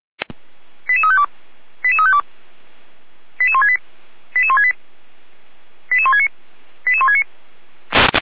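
Five-tone selective call (ZVEI), heard as quick runs of five tones, tones of about 70 ms. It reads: 88352 88352 88168 88168 88169 88169
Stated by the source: ZVEI five-tone selective call received with an SDR.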